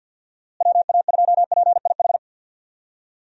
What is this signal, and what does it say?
Morse code, heard as a keyed tone, about 45 wpm, 700 Hz. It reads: WA2PIH
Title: USA Call Signs - 6 Characters - Repeated 3x - 50wpm